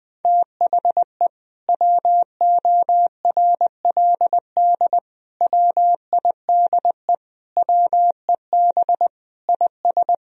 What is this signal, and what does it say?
Morse code, heard as a keyed tone, about 19 wpm, 700 Hz. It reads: THE WORLD WIDE WEB IS
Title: Sentences from Top 1000 Words - Review Part 4 - 20wpm